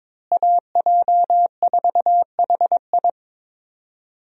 Morse code, heard as a keyed tone, about 22 wpm, 700 Hz. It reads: AJ4HI
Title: USA Call Signs - 5 Characters - Repeated 3x - 22wpm